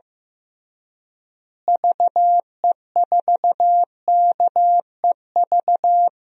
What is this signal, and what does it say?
Morse code, heard as a keyed tone, about 15 wpm, 700 Hz. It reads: VE4KEV
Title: DXCC-100 Call Signs - 6 Characters - Repeated 3x - 15wpm